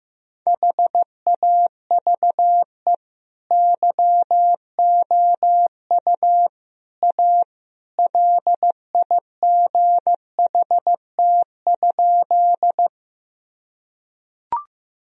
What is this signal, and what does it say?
Morse code, heard as a keyed tone, about 15 wpm, 700 Hz. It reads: HAVE YOU A LIGHT?
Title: Sentences from Top 1000 Words - Review Part 1 - 15wpm